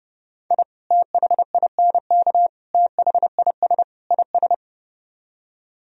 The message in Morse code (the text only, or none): I THINK THIS IS